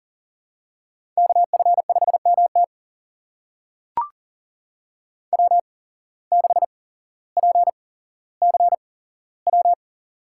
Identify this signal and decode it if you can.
Morse code, heard as a keyed tone, about 39 wpm, spaced as for 17 wpm, 700 Hz. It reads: KF5MT W6PCW
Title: USA Call Signs - 5 Characters - 8x Character Spacing - 40wpm